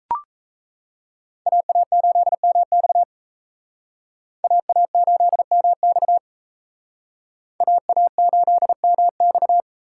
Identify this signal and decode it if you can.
Morse code, heard as a keyed tone, about 35 wpm, 700 Hz. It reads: AA8MX AA8MX AA8MX
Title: Speed Racing - USA 2x2 Call Signs 28wpm